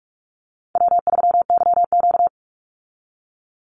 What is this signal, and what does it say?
Morse code, heard as a keyed tone, about 46 wpm, 700 Hz. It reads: W3YQ
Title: Easy Call Signs Repeated 3x - 45wpm